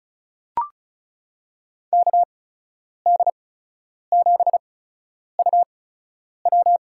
Morse code, heard as a keyed tone, about 36 wpm, spaced as for 10 wpm, 700 Hz. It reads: KD7UW
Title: USA Call Signs - 8x Character Spacing - 35wpm